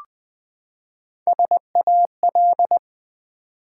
Morse code, heard as a keyed tone, about 20 wpm, 700 Hz. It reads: SAL
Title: Build A Word - 3 Letters At A Time - 6-Letter Words - 20wpm